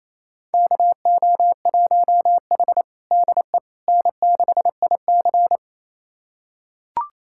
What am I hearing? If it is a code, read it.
Morse code, heard as a keyed tone, about 28 wpm, 700 Hz. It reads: KO1H DE N6IC
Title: USA Callsign DE Callsign - 4 Characters - 28wpm